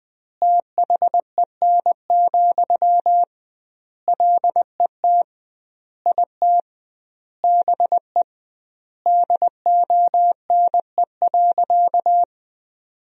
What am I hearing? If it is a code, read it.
Morse code, heard as a keyed tone, about 20 wpm, 700 Hz. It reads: THEN, LET IT BE DONE.
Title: 2x Word-Spacing - Sentences from Top 500 Words - 20wpm